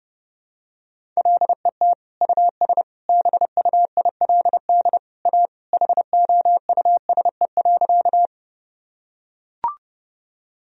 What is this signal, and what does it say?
Morse code, heard as a keyed tone, about 29 wpm, 700 Hz. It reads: LET US BUILD A HOUSE.
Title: Sentences from Top 1000 Words - Review Part 1 - 30wpm